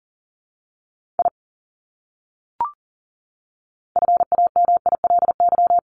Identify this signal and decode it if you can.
Morse code, heard as a keyed tone, about 37 wpm, 700 Hz. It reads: I FAMILY
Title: Top 500 Words Repeated 40wpm